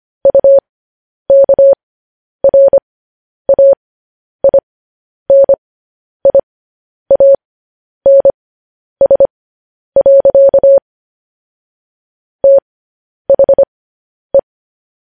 Morse code, heard as a keyed tone, about 25 wpm, spaced as for 10 wpm, 550 Hz. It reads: UKRAINIANS. THE